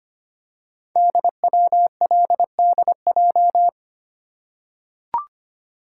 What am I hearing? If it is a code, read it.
Morse code, heard as a keyed tone, about 25 wpm, 700 Hz. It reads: DWLDJ
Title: Code Groups - 5 Letters - 25wpm